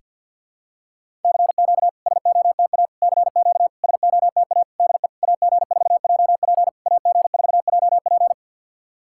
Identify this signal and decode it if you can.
Morse code, heard as a keyed tone, about 52 wpm, 690 Hz. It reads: CQ SOTA CQ SOTA DE AG4JP AG4JP
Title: SOTA QSO Practice 50wpm